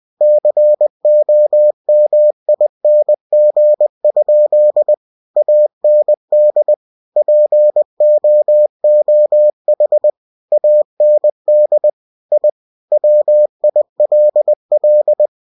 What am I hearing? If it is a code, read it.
Morse code, heard as a keyed tone, about 20 wpm, 600 Hz. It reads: COMING? AND POOH AND I WILL
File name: cw-books / winnie-the-pooh / winnie-20wpm-page-0098